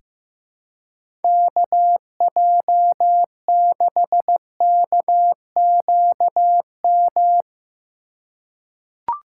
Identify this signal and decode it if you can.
Morse code, heard as a keyed tone, about 15 wpm, 700 Hz. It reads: KJ6KQM
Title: USA Call Signs - 6 Characters - 3x Character Spacing - 15wpm